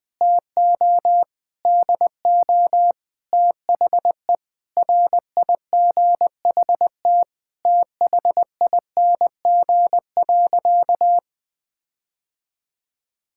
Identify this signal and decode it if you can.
Morse code, heard as a keyed tone, about 20 wpm, 700 Hz. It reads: TO DO THE RIGHT THING.
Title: Sentences with 5-Word Phrases - 20wpm